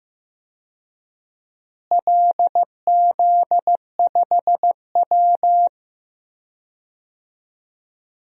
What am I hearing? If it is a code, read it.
Morse code, heard as a keyed tone, about 15 wpm, 700 Hz. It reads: LZ5W
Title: DXCC-100 Call Signs - 4 Characters - Repeated 2x - 15wpm